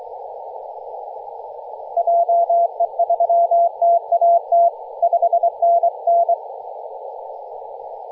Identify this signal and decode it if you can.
Morse code, heard as a keyed tone, about 23 wpm, 680 Hz, with radio noise. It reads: JE3TAT 5NN
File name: posts / 2016-nov-13-0348z-t30jy-28-034mhz-cw